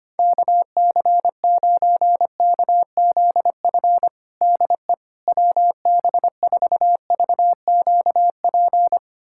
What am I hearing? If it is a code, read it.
Morse code, heard as a keyed tone, about 25 wpm, 700 Hz. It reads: KC9KZF DE WB4VQP
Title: USA Callsign DE Callsign - 6 Characters - 25wpm